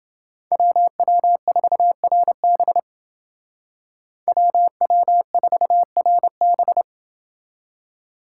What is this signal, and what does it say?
Morse code, heard as a keyed tone, about 29 wpm, 700 Hz. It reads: WW4RB WW4RB